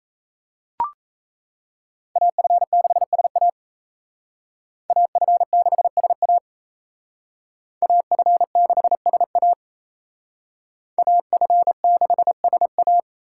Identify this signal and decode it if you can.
Morse code, heard as a keyed tone, about 36 wpm, 700 Hz. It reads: AF6SA AF6SA AF6SA AF6SA